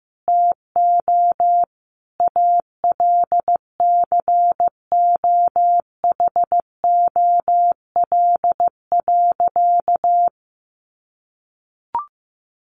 Morse code, heard as a keyed tone, about 15 wpm, 700 Hz. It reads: TO ALCOHOL.
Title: Sentences from Top 1400 Words 15wpm